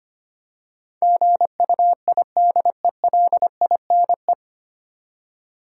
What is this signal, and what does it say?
Morse code, heard as a keyed tone, about 25 wpm, 700 Hz. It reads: GUIDELINE
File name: practice / 25wpm / 9 Letter Words - First 3 Letters Given - 25wpm